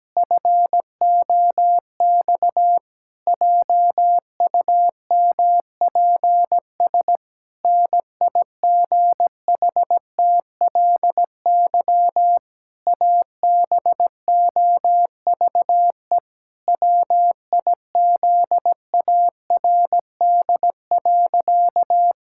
Morse code, heard as a keyed tone, about 17 wpm, 700 Hz. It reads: FOX JUMPS NIGHTLY ABOVE WIZARD.
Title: Pangram Challenge v2 - 17wpm